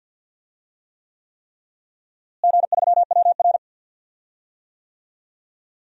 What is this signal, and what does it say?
Morse code, heard as a keyed tone, about 50 wpm, 700 Hz. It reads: G3WR
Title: DXCC-100 Call Signs - 4 Characters - Repeated 3x - 50wpm